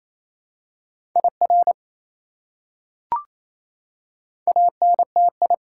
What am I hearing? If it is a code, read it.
Morse code, heard as a keyed tone, about 28 wpm, 700 Hz. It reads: IR ANTI
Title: Prefixes and Suffixes 28wpm